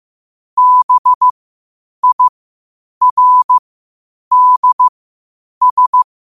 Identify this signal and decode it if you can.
Morse code, heard as a keyed tone, about 15 wpm, spaced as for 8 wpm, 1000 Hz. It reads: BIRDS
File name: qotdfiles / 8wpm